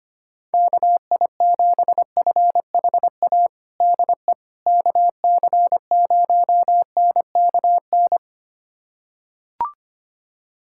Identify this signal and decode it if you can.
Morse code, heard as a keyed tone, about 25 wpm, 700 Hz. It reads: KI7FHA DE KC0NKN
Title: USA Callsign DE Callsign 25wpm